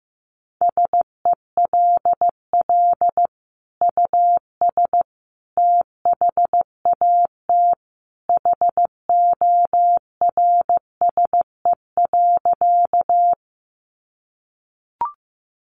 Morse code, heard as a keyed tone, about 15 wpm, 700 Hz. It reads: SELL US THAT HORSE.
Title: Sentences from Top 1100 Words 15wpm